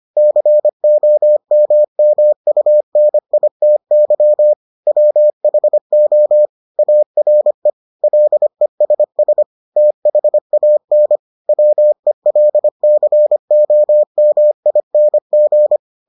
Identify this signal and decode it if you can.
Morse code, heard as a keyed tone, about 25 wpm, 600 Hz. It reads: COMMUNITY WHO ARE LESS THAN WELCOMING